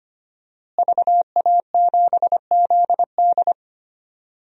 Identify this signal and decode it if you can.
Morse code, heard as a keyed tone, about 25 wpm, 700 Hz. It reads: VA7ZD
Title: DXCC-100 Call Signs - 5 Characters - Repeated 3x - 25wpm